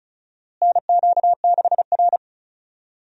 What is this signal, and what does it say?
Morse code, heard as a keyed tone, about 36 wpm, 700 Hz. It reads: NQ6R
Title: USA Call Signs - 4 Characters - Repeated 2x - 35wpm